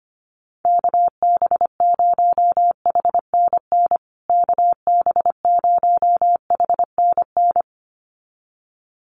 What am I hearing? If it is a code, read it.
Morse code, heard as a keyed tone, about 25 wpm, 700 Hz. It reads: KB0HNN KB0HNN
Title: USA Call Signs Repeated 2x 25wpm